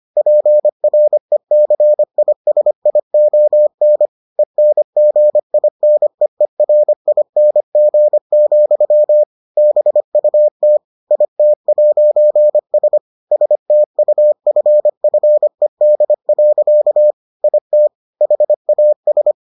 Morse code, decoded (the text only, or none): PRECISION ENGINEERING, BUT IT'S STUFFED. IT HAS